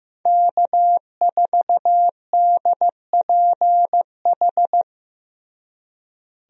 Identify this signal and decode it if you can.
Morse code, heard as a keyed tone, about 15 wpm, 700 Hz. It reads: K4DPH
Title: USA Call Signs - 5 Characters - Repeated 3x - 15wpm